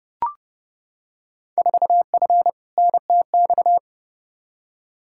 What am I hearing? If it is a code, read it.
Morse code, heard as a keyed tone, about 30 wpm, 700 Hz. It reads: 4F NTX
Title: Field Day Exchanges Encore - 30wpm